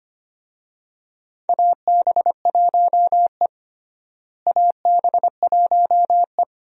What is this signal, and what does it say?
Morse code, heard as a keyed tone, about 25 wpm, 700 Hz. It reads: AB1E AB1E
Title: USA Call Signs - 4 Characters - Repeated 3x - 25wpm